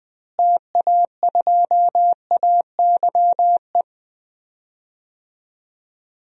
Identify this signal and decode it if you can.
Morse code, heard as a keyed tone, about 20 wpm, 700 Hz. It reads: TA2AYE
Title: DXCC-100 Call Signs - 6 Characters - Repeated 2x - 20wpm